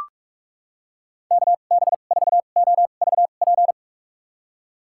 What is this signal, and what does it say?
Morse code, heard as a keyed tone, about 41 wpm, spaced as for 31 wpm, 700 Hz. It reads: KB4OVP
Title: USA Call Signs - 2x Character Spacing - 45wpm